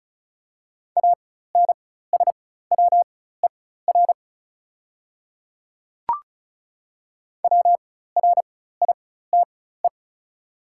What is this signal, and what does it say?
Morse code, heard as a keyed tone, about 34 wpm, spaced as for 16 wpm, 700 Hz. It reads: ANSWER WRITE
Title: 4x Character Spacing - Top 300 Words - 35wpm